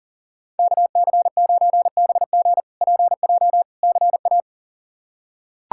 Morse code, heard as a keyed tone, about 37 wpm, 700 Hz. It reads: KC9BG PJ CA
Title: K1USN SST QSO Practice - 40wpm